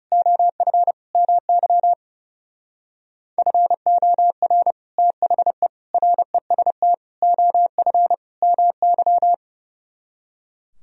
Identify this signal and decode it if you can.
Morse code, heard as a keyed tone, about 38 wpm, 700 Hz. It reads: OF MY FOR THE REST OF MY